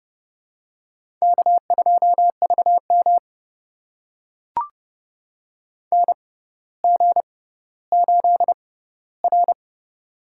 Morse code, heard as a keyed tone, about 30 wpm, spaced as for 15 wpm, 700 Hz. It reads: K2VM NG8R
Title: USA Call Signs - 4 Characters - 6x Character Spacing - 30wpm